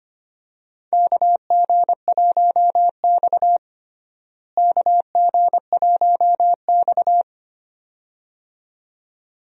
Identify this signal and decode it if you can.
Morse code, heard as a keyed tone, about 25 wpm, 700 Hz. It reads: KG1X KG1X
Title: USA Call Signs - 4 Characters - Repeated 2x - 25wpm